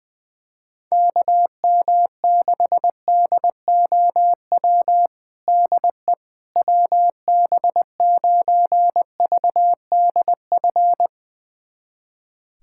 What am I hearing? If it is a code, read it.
Morse code, heard as a keyed tone, about 19 wpm, 700 Hz. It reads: KM6DOW DE WB9VDF
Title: USA Callsign DE Callsign - 6 Characters - 20wpm